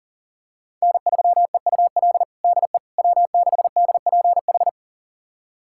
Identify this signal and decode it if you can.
Morse code, heard as a keyed tone, about 40 wpm, 700 Hz. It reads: N3EUL DE W6DPH
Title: USA Callsign DE Callsign - 5 Characters - 40wpm